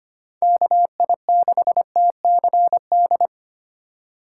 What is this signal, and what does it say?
Morse code, heard as a keyed tone, about 31 wpm, 700 Hz. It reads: KI6TCD
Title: Speed Racing - USA Call Signs 25wpm